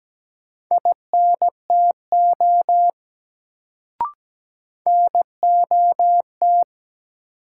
Morse code, heard as a keyed tone, about 17 wpm, 700 Hz. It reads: INTO NOT
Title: Sentences from Top 600 Words - New Word Drill - 17wpm